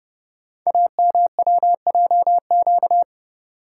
Morse code, heard as a keyed tone, about 30 wpm, 700 Hz. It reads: AMWJQ
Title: Code Groups - 5 Letters - 30wpm